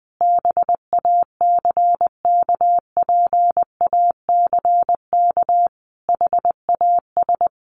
Morse code, heard as a keyed tone, about 19 wpm, 700 Hz. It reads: BACKPACK HAS
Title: Additional Easy Phrases 20wpm